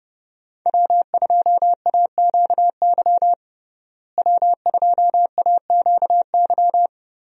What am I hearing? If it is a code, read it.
Morse code, heard as a keyed tone, about 30 wpm, 700 Hz. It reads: W2AQY W2AQY
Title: USA Call Signs - 5 Characters - Repeated 3x - 30wpm